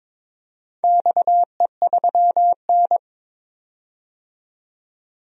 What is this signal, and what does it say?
Morse code, heard as a keyed tone, about 22 wpm, 700 Hz. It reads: XE3N
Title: DXCC-100 Call Signs - 4 Characters - 22wpm